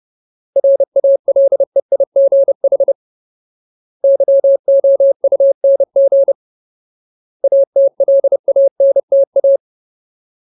Morse code, heard as a keyed tone, about 30 wpm, 550 Hz. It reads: RALEIGH YOUNG ATLANTA
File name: Ultimate Word List / 30WPM Ultimate